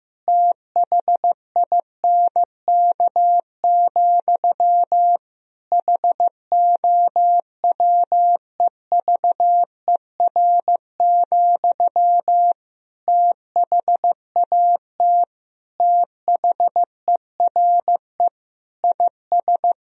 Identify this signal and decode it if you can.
Morse code, heard as a keyed tone, about 15 wpm, 700 Hz. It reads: THINK, HOWEVER, THAT THERE IS
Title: Sentences from Top 1000 Words - Review Part 3 - 15wpm